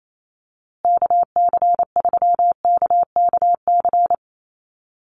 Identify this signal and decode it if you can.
Morse code, heard as a keyed tone, about 28 wpm, 700 Hz. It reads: KC3KKC
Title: USA Call Signs Repeated 3x 28wpm